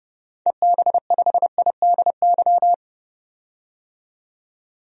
Morse code, heard as a keyed tone, about 32 wpm, 700 Hz. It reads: EB5IDY
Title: DXCC-100 Call Signs - 6 Characters - 30wpm